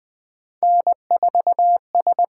Morse code, heard as a keyed tone, about 19 wpm, 700 Hz. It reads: N 4 S
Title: USA Call Signs - 5 Characters - 8x Character Spacing - 20wpm